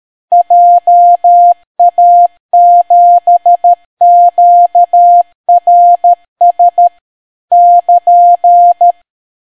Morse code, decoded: JA7QRS <KN>